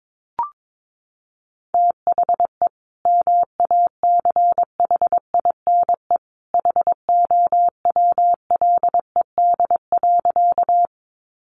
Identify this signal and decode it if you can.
Morse code, heard as a keyed tone, about 22 wpm, 700 Hz. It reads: THE MACHINE HOWLED.
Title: A Sound Of Thunder 22wpm